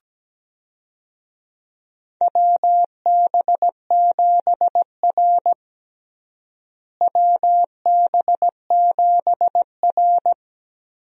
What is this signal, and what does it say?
Morse code, heard as a keyed tone, about 17 wpm, 700 Hz. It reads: WB7R WB7R
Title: USA Call Signs - 4 Characters - Repeated 2x - 17wpm